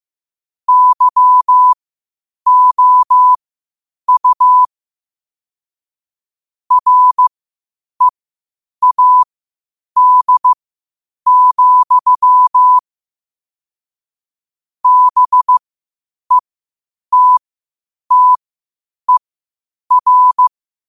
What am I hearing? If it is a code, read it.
Morse code, heard as a keyed tone, about 15 wpm, spaced as for 8 wpm, 1000 Hz. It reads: YOU READ, BETTER
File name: qotdfiles / 8wpm